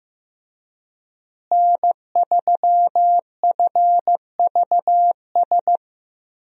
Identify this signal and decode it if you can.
Morse code, heard as a keyed tone, about 15 wpm, 700 Hz. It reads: N3FVS
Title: USA Call Signs - 5 Characters - Repeated 2x - 15wpm